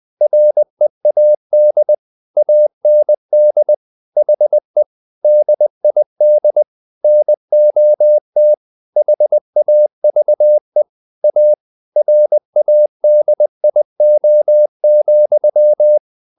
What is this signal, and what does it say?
Morse code, heard as a keyed tone, about 20 wpm, 600 Hz. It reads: READ AND HE DID NOT HAVE A RADIO,
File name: the old man and the sea part 2 20wpm